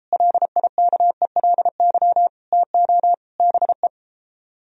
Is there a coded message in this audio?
Yes: Morse code from 0.1 to 3.9 s, heard as a keyed tone, about 33 wpm, 700 Hz. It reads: LIKELY TO BE